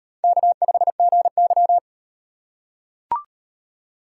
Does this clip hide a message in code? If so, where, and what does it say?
Yes: Morse code from 0.2 to 1.8 s, heard as a keyed tone, about 38 wpm, 700 Hz. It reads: K5GY